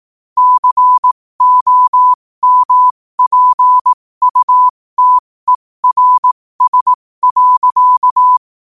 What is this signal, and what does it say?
Morse code, heard as a keyed tone, about 18 wpm, spaced as for 15 wpm, 1000 Hz. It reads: COMPUTERS.
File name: qotdfiles / 15wpm